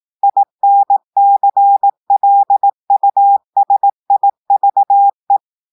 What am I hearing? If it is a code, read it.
Morse code, heard as a keyed tone, about 18 wpm, 800 Hz. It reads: INCLUSIVE
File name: audios / portugais / portugais texte en clair 2 vitesse 18 mots minutes